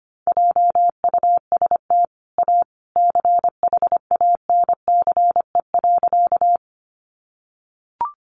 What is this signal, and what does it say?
Morse code, heard as a keyed tone, about 25 wpm, 700 Hz. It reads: JUST A CHANCE.
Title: Sentences from Top 800 Words 25wpm